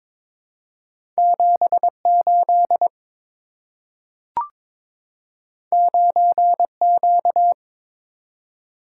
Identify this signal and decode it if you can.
Morse code, heard as a keyed tone, about 23 wpm, 700 Hz. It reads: 78 9Q
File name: Two Letter-Number Combinations - 22wpm